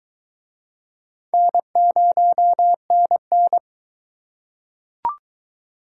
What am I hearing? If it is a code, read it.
Morse code, heard as a keyed tone, about 22 wpm, 700 Hz. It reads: N0NN